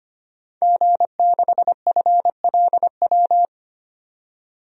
Morse code, heard as a keyed tone, about 25 wpm, 700 Hz. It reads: G6FLW
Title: DXCC-100 Call Signs - 5 Characters - 25wpm